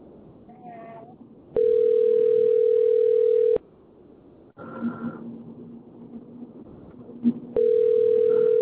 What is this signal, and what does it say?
Ringback tone (North America), 440 + 480 Hz.